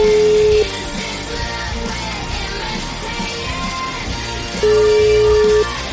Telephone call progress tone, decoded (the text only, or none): ringback tone